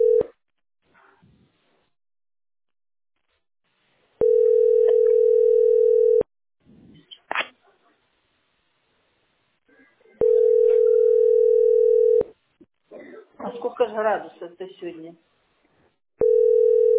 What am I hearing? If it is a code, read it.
Ringback tone (North America), 440 + 480 Hz, with speech.
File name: fbd33465-f072-44fb-aeff